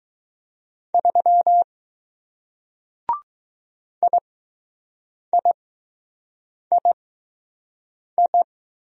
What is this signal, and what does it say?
Morse code, heard as a keyed tone, about 20 wpm, spaced as for 6 wpm, 700 Hz. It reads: 3 IIII